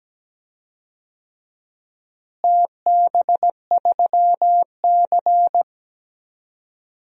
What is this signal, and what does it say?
Morse code, heard as a keyed tone, about 17 wpm, 700 Hz. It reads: TB3C
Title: DXCC-100 Call Signs - 4 Characters - Repeated 2x - 17wpm